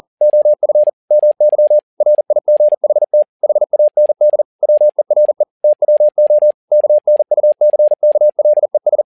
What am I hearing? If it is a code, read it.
Morse code, heard as a keyed tone, about 40 wpm, 600 Hz. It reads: OF MY RIGHT HAND WERE TWO KNUCKLES
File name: learning center / tower trouble / tower touble 40wpm